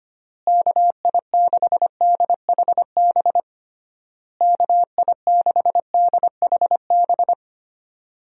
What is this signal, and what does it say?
Morse code, heard as a keyed tone, about 25 wpm, 700 Hz. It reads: KI6DHB KI6DHB
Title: USA Call Signs - 6 Characters - Repeated 3x - 25wpm